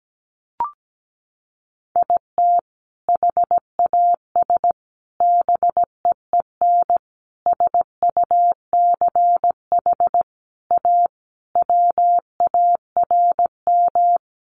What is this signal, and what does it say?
Morse code, heard as a keyed tone, about 17 wpm, 700 Hz. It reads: IT HAS BEEN SUCH A WARM
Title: Sentences from Top 1100 Words 17wpm